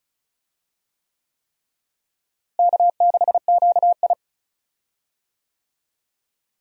Morse code, heard as a keyed tone, about 36 wpm, 700 Hz. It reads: K6QI
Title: USA Call Signs - 4 Characters - Repeated 3x - 35wpm